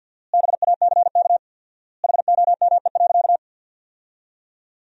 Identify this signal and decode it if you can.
Morse code, heard as a keyed tone, about 52 wpm, 700 Hz. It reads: BACK HOME.
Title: 4x Word-Spacing - Sentences from Top 500 Words - 50wpm